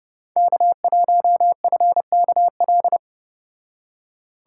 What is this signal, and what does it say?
Morse code, heard as a keyed tone, about 30 wpm, 700 Hz. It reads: K1FKL